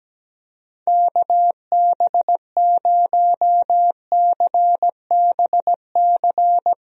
Morse code, heard as a keyed tone, about 17 wpm, 700 Hz. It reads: KB0CBC
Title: USA Call Signs - 6 Characters - Repeated 2x - 17wpm